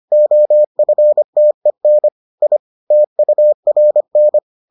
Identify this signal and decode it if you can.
Morse code, heard as a keyed tone, about 25 wpm, 600 Hz. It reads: OFTEN I TURN